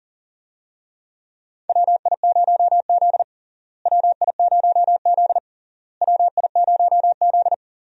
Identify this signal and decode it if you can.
Morse code, heard as a keyed tone, about 37 wpm, 700 Hz. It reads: WI0Z WI0Z WI0Z
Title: USA Call Signs - 4 Characters - Repeated 3x - 40wpm